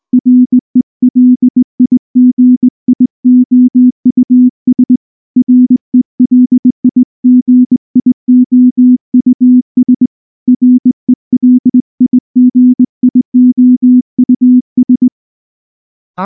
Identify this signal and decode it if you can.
Morse code, heard as a keyed tone, about 19 wpm, 250 Hz. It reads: RELIGIOUS RELIGIOUS RELIGIOUS